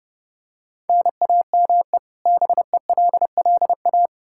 Morse code, heard as a keyed tone, about 30 wpm, 700 Hz. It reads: NAME BELLA
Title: Practice QSO - Exchange 1 - Beginning - 30wpm